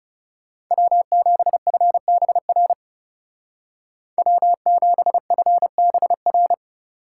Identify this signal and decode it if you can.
Morse code, heard as a keyed tone, about 36 wpm, 700 Hz. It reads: W7FBR W7FBR
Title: Speed Racing - USA 1x3 Call Signs 30wpm